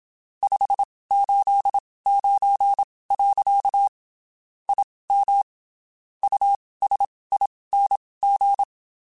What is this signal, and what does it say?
Morse code, heard as a keyed tone, about 27 wpm, spaced as for 19 wpm, 800 Hz. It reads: 589. IM USING
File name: QSOs Spaced